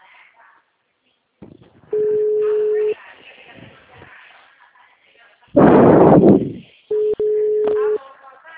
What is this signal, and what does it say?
Ringback tone (Europe, 425 Hz), with speech.
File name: out- -2050- - - .49388